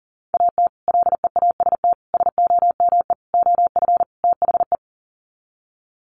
Morse code, heard as a keyed tone, about 40 wpm, 700 Hz. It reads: AT LEAST SOME OF THE
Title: Sets of 5 Words Encore 40wpm